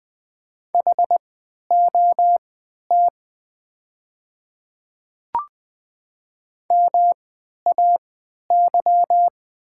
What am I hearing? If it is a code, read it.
Morse code, heard as a keyed tone, about 20 wpm, spaced as for 11 wpm, 700 Hz. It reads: HOT MAY
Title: 3x Character Spacing - Top 100 Words - 20wpm